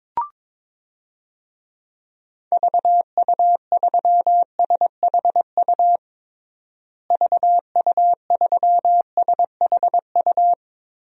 Morse code, heard as a keyed tone, about 22 wpm, 700 Hz. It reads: VU3SHU VU3SHU
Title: DXCC-100 Call Signs - 6 Characters - Repeated 2x - 22wpm